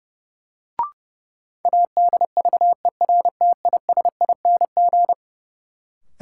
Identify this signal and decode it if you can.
Morse code, heard as a keyed tone, about 32 wpm, 700 Hz. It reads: ADVERTISING
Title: Sentences from Top 1000 Words - Review Part 6 - New Word Drill - 30wpm